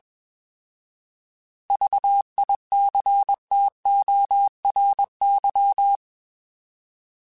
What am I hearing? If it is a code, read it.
Morse code, heard as a keyed tone, about 21 wpm, 800 Hz. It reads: VICTORY
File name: Alphabetical / Alphabetical mp3s / Vs @20